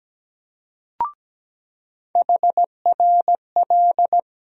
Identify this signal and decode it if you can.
Morse code, heard as a keyed tone, about 17 wpm, 700 Hz. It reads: HRL
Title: USA Call Sign Postfixes - 3 Letters 17wpm